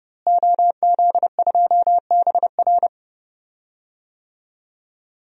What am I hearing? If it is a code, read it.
Morse code, heard as a keyed tone, about 30 wpm, 700 Hz. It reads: OZ2BR
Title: DXCC-100 Call Signs - 5 Characters - Repeated 3x - 30wpm